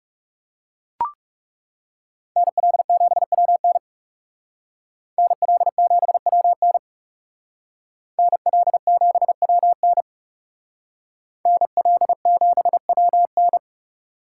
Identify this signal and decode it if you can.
Morse code, heard as a keyed tone, about 39 wpm, 700 Hz. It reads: NL7WN NL7WN NL7WN NL7WN